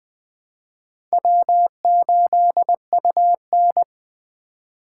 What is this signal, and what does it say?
Morse code, heard as a keyed tone, about 23 wpm, 700 Hz. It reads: W8UN
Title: Speed Racing - USA Call Signs 20wpm